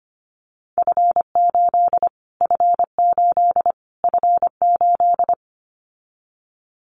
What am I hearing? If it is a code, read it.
Morse code, heard as a keyed tone, about 25 wpm, 700 Hz. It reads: F8 F8 F8
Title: Callsign Prefixes for Top 25 DXCC - Repeated 3x - 25wpm